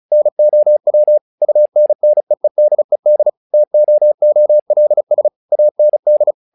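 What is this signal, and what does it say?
Morse code, heard as a keyed tone, about 34 wpm, 600 Hz. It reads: NOW UNNEEDED TOOLS AND
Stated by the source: Morse practice text "Tower Trouble" sent at 35 WPM, from the SKCC - Straight Key Century Club learning center audio files.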